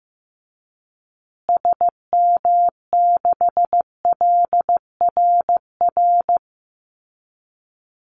Morse code, heard as a keyed tone, about 15 wpm, 700 Hz. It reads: SM6LRR
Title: Call Signs Repeated 3x - 15wpm